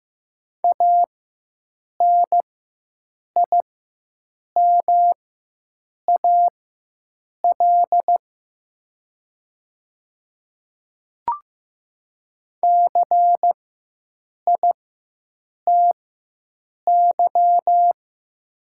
Morse code, heard as a keyed tone, about 15 wpm, spaced as for 7 wpm, 700 Hz. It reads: ANIMAL CITY
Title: 4x Character Spacing - Top 300 Words - 15wpm